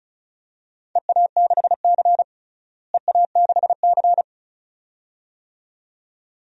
Morse code, heard as a keyed tone, about 36 wpm, 700 Hz. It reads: EA6C EA6C
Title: DXCC-100 Call Signs - 4 Characters - Repeated 2x - 35wpm